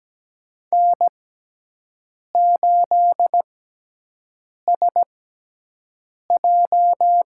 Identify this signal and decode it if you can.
Morse code, heard as a keyed tone, about 17 wpm, spaced as for 6 wpm, 700 Hz. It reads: N8SJ
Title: USA Call Signs - 4 Characters - 6x Character Spacing - 17wpm